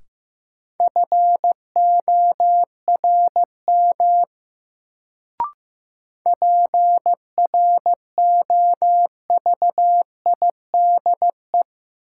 Morse code, heard as a keyed tone, about 15 wpm, 700 Hz. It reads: FORM PROVIDE
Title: Sentences from Top 1100 Words - New Word Drill Alt - 15wpm